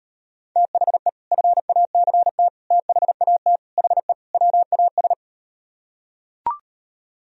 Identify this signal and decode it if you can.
Morse code, heard as a keyed tone, about 38 wpm, 700 Hz. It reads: THE FACT THAT HE WAS